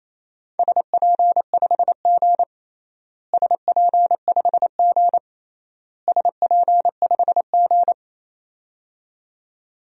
Morse code, heard as a keyed tone, about 28 wpm, 700 Hz. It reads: SP5G SP5G SP5G
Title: DXCC-100 Call Signs - 4 Characters - Repeated 3x - 28wpm